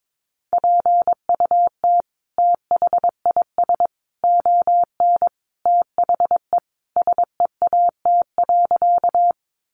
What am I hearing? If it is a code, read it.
Morse code, heard as a keyed tone, about 22 wpm, 700 Hz. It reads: PUT THIS ON THE SEAT.
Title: Sentences from Top 1500 Words 22wpm